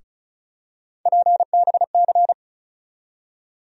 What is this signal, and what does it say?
Morse code, heard as a keyed tone, about 36 wpm, 700 Hz. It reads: PBC
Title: USA Call Sign Postfixes - 3 Letters 35wpm